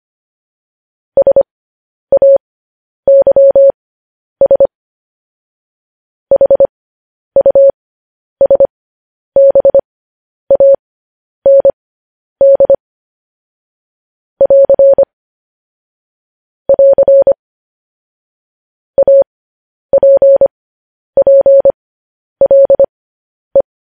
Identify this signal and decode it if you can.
Morse code, heard as a keyed tone, about 25 wpm, spaced as for 10 wpm, 550 Hz. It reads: SAYS HUSBAND <AR> <AR> APPLE
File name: podcast / News headlines-1x-25x10